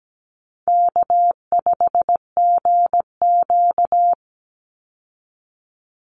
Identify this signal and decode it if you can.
Morse code, heard as a keyed tone, about 17 wpm, 700 Hz. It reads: K5GQ
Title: Call Signs Repeated 3x - 17wpm